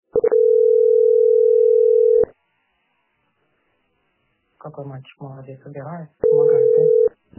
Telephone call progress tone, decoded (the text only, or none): ringback tone